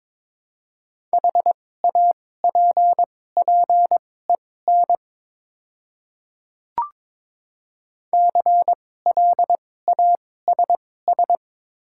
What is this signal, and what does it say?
Morse code, heard as a keyed tone, about 22 wpm, spaced as for 16 wpm, 700 Hz. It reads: HAPPEN CLASS
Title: 2x Character Spacing - Top 500 Words - 22wpm